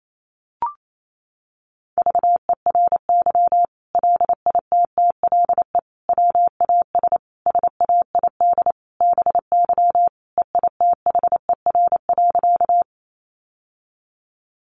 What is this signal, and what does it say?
Morse code, heard as a keyed tone, about 28 wpm, 700 Hz. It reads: VERY LITTLE WAS SAID BY EITHER.
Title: Sentences from Top 600 Words 28wpm